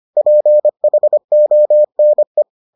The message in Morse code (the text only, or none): PHONE